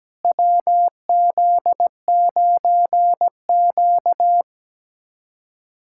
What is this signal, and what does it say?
Morse code, heard as a keyed tone, about 17 wpm, 700 Hz. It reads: WZ9Q
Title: USA Callsign DE Callsign - 4 Characters - 17wpm